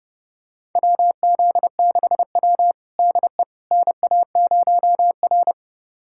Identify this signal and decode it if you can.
Morse code, heard as a keyed tone, about 30 wpm, 700 Hz. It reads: WZ6W DE NA0R
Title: USA Callsign DE Callsign - 4 Characters - 30wpm